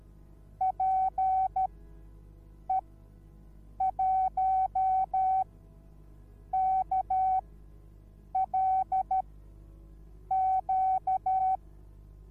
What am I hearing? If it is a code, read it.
Morse code, heard as a keyed tone, about 13 wpm, spaced as for 6 wpm, 760 Hz. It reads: PE1KLQ